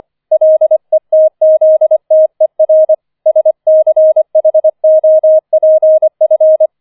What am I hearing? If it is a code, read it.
Morse code, heard as a keyed tone, about 25 wpm, 630 Hz. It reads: LETZTER SCHOPF
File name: 004 Vierter Streich 25wpm